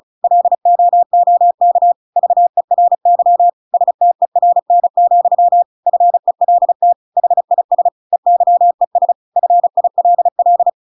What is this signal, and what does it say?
Morse code, heard as a keyed tone, about 34 wpm, 700 Hz. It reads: LOOK VERY STERN, FELT HIS EYES FILL
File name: Pinocchio / Pinocchio ch8 35 35 ik1hge